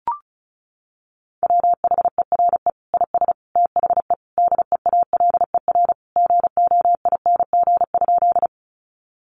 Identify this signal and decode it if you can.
Morse code, heard as a keyed tone, about 34 wpm, 700 Hz. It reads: WHERE IS THE DEALER GOING?
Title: Sentences from Top 1100 Words 35wpm